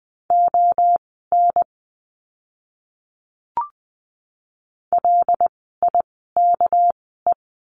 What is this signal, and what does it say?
Morse code, heard as a keyed tone, about 20 wpm, spaced as for 14 wpm, 700 Hz. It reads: ON LIKE
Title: 199 Lesson 32f - Letter k - Words 20wpm